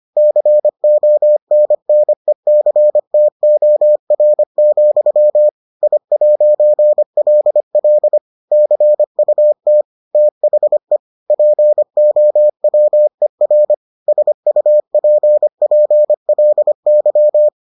Morse code, heard as a keyed tone, about 25 wpm, 600 Hz. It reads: CONNECTOR, I'LL CUT THE POWER SUPPLY